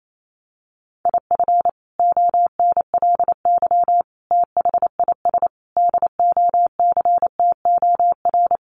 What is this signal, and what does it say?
Morse code, heard as a keyed tone, about 28 wpm, 700 Hz. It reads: IF ONLY THIS DOCTOR
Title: Sentences from Top 1500 Words 28wpm